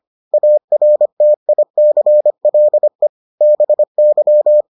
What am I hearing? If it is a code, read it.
Morse code, heard as a keyed tone, about 25 wpm, 600 Hz. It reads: ARTICLE BY